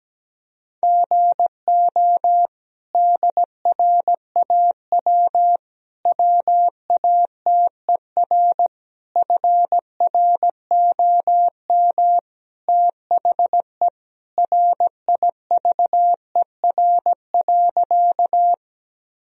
Morse code, heard as a keyed tone, about 17 wpm, 700 Hz. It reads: GO DRAW WATER FROM THE RIVER.
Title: Sentences from Top 1000 Words - Review Part 1 - 17wpm